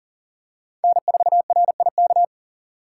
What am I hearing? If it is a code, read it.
Morse code, heard as a keyed tone, about 40 wpm, 700 Hz. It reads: N4RIK
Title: USA Call Signs - 6x Character Spacing - 40wpm